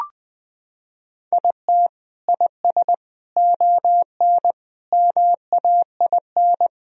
Morse code, heard as a keyed tone, about 21 wpm, 700 Hz. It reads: IT IS ON MAIN
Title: Sentences from Top 1000 Words - Review Part 2 - 20wpm